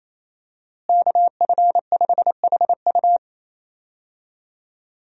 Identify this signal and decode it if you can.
Morse code, heard as a keyed tone, about 28 wpm, 700 Hz. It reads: KF5HU
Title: USA Call Signs - 5 Characters - Repeated 2x - 28wpm